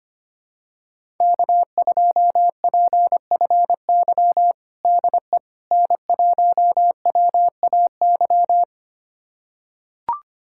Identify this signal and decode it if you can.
Morse code, heard as a keyed tone, about 25 wpm, 700 Hz. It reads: K2PFY DE N1WAY
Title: USA Callsign DE Callsign 25wpm